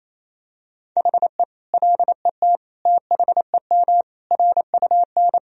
Morse code, heard as a keyed tone, about 28 wpm, 700 Hz. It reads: HE LET THEM RUN
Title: Sentences from Top 1000 Words - Review Part 3 - 28wpm